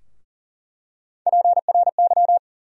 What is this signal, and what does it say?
Morse code, heard as a keyed tone, about 40 wpm, 700 Hz. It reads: PRY
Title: USA Call Sign Postfixes - 3 Letters 40wpm